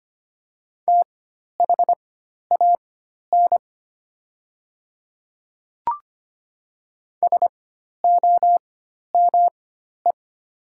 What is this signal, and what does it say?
Morse code, heard as a keyed tone, about 25 wpm, spaced as for 12 wpm, 700 Hz. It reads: THAN SOME
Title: 4x Character Spacing - Top 200 Words - 25wpm